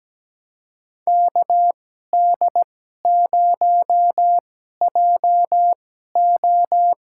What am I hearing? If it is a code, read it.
Morse code, heard as a keyed tone, about 17 wpm, spaced as for 12 wpm, 700 Hz. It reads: KD0JO
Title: USA Call Signs - 2x Character Spacing - 17wpm